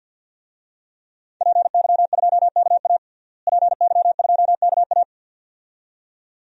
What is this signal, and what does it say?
Morse code, heard as a keyed tone, about 52 wpm, 700 Hz. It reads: PY2KA PY2KA
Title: DXCC-100 Call Signs - 5 Characters - Repeated 2x - 50wpm